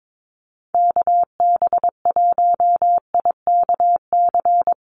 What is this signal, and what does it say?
Morse code, heard as a keyed tone, about 22 wpm, 700 Hz. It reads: KB1IKC
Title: USA Call Signs 22wpm